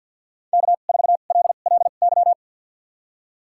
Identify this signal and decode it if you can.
Morse code, heard as a keyed tone, about 52 wpm, spaced as for 36 wpm, 700 Hz. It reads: K4LLY
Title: USA Call Signs - 5 Characters - 2x Character Spacing - 50wpm